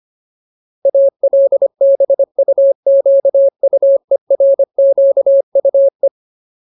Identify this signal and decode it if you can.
Morse code, heard as a keyed tone, about 25 wpm, 550 Hz. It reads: ALBUQUERQUE
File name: Ultimate Word List / 25WPM Ultimate